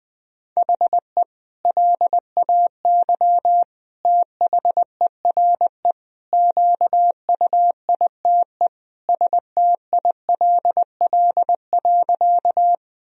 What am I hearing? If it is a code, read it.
Morse code, heard as a keyed tone, about 20 wpm, 700 Hz. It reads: HE LAY THERE QUITE STILL.
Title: Sentences from Top 1000 Words - Review Part 2 - 20wpm